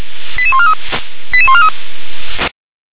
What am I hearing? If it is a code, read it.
Five-tone selective call (ZVEI), heard as quick runs of five tones, tones of about 70 ms. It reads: 88143 88143